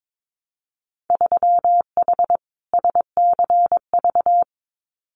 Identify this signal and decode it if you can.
Morse code, heard as a keyed tone, about 22 wpm, 700 Hz. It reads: 3H SCV
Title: Winter Field Day Exchanges Encore 22wpm